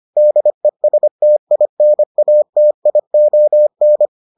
Morse code, heard as a keyed tone, about 25 wpm, 600 Hz. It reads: DESTINATION